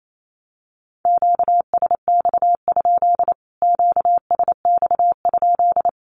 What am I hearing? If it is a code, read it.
Morse code, heard as a keyed tone, about 28 wpm, 700 Hz. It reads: QSX? QSX?
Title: Q-Codes Repeated 3x 28wpm